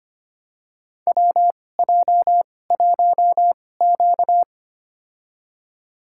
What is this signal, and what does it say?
Morse code, heard as a keyed tone, about 25 wpm, spaced as for 18 wpm, 700 Hz. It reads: WJ1Q
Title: USA Call Signs - 2x Character Spacing - 25wpm